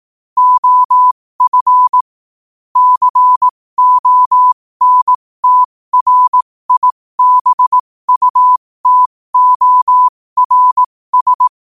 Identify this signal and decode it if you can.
Morse code, heard as a keyed tone, about 18 wpm, spaced as for 15 wpm, 1000 Hz. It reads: OF CONTRIBUTORS
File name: qotdfiles / 15wpm